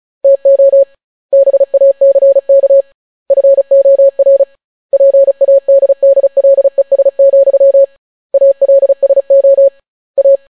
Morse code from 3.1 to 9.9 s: FOR PADDLES, ALSO